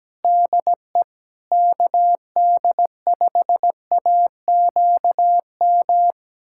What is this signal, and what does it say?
Morse code, heard as a keyed tone, about 17 wpm, 700 Hz. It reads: DE KD5AQM
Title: USA Callsign DE Callsign 17wpm